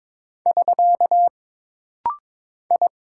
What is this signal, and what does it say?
Morse code, heard as a keyed tone, about 22 wpm, 700 Hz. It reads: <SK> I
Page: This rapid-fire timing is geared to push your ICR (Instant Character Recognition) skills.